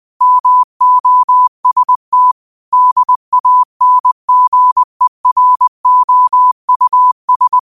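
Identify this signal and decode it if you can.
Morse code, heard as a keyed tone, about 20 wpm, 1000 Hz. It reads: MOST DANGEROUS
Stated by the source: 20 WPM Morse code quotes, QOTD at 20 WPM.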